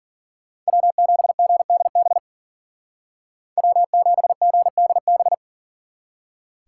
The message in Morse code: W7GDB W7GDB

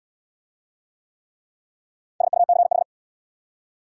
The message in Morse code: SUCH

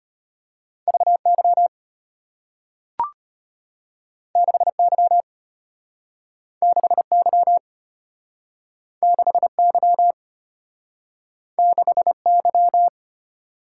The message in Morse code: VY 6Y 6Y 6Y 6Y

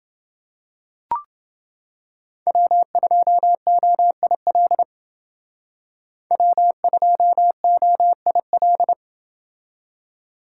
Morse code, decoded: W2OIL W2OIL

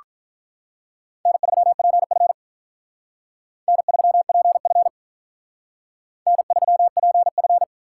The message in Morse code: N3PF N3PF N3PF

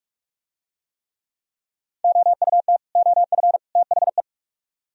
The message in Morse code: OUT OF THE

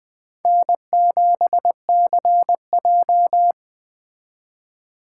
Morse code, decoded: N7CJ

